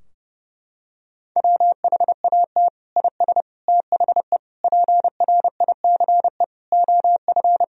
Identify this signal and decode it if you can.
Morse code, heard as a keyed tone, about 31 wpm, 700 Hz. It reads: WHAT IS THE PRICE OF
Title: Sentences from Top 1000 Words - Review Part 4 - 30wpm